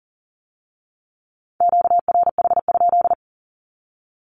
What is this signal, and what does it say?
Morse code, heard as a keyed tone, about 40 wpm, 700 Hz. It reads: QRH?
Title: Q-Codes with Spoken Meaning - 40wpm